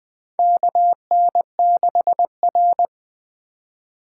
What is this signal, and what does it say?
Morse code, heard as a keyed tone, about 20 wpm, 700 Hz. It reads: KN6R
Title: USA Call Signs - 4 Characters - 2x Character Spacing - 20wpm